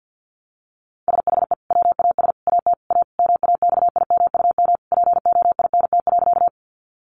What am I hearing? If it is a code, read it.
Morse code, heard as a keyed tone, about 50 wpm, 700 Hz. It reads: SHE WAS AT A MAXIMUM POINT.
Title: Sentences from Top 1300 Words 50wpm